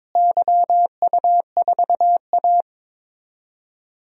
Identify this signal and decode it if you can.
Morse code, heard as a keyed tone, about 22 wpm, 700 Hz. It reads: YU4A
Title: DXCC-100 Call Signs - 4 Characters - Repeated 3x - 22wpm